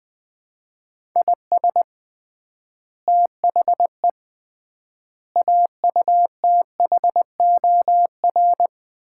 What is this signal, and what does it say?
Morse code, heard as a keyed tone, about 21 wpm, 700 Hz. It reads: IS THE AUTHOR
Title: 3x Word-Spacing - Sets of 4 Words - 20wpm